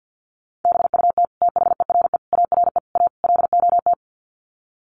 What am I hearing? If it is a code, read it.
Morse code, heard as a keyed tone, about 50 wpm, 700 Hz. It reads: BUT THERE ARE A LOT
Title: Sets of 5 Words Encore2 50wpm